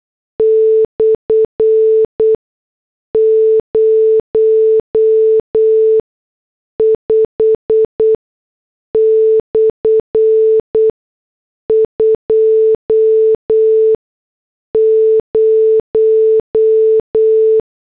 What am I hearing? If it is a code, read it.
Morse code, heard as a keyed tone, about 8 wpm, spaced as for 6 wpm, 440 Hz. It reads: /05/20